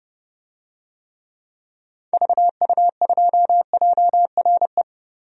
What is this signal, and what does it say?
Morse code, heard as a keyed tone, about 32 wpm, 700 Hz. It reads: VU2JRE
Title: DXCC-100 Call Signs - 6 Characters - Repeated 3x - 30wpm